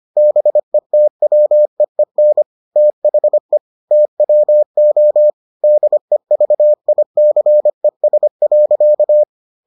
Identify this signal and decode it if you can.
Morse code, heard as a keyed tone, about 25 wpm, 600 Hz. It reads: BETWEEN THE TWO DEVICES.